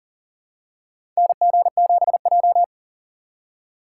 Morse code, heard as a keyed tone, about 40 wpm, 700 Hz. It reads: NG7J
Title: USA Call Signs - 4 Characters - 3x Character Spacing - 40wpm